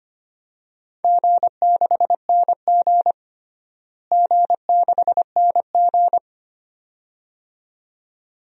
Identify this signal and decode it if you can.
Morse code, heard as a keyed tone, about 25 wpm, 700 Hz. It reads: G6NG G6NG
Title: DXCC-100 Call Signs - 4 Characters - Repeated 2x - 25wpm